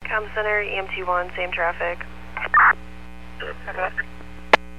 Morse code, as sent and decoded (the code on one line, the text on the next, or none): none